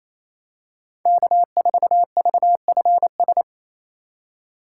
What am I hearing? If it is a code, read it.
Morse code, heard as a keyed tone, about 28 wpm, 700 Hz. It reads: K4VFS
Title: USA Call Signs - 8x Character Spacing - 28wpm